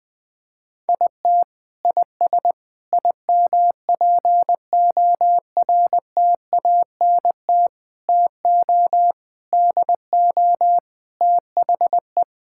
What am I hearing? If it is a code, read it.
Morse code, heard as a keyed tone, about 20 wpm, 700 Hz. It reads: IT IS IMPORTANT TO DO THE
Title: Sentences with 5-Word Phrases - 20wpm